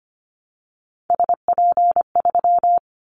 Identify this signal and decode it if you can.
Morse code, heard as a keyed tone, about 25 wpm, 700 Hz. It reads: SP3
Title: Callsign Prefixes for Top 25 DXCC - Repeated 3x - 25wpm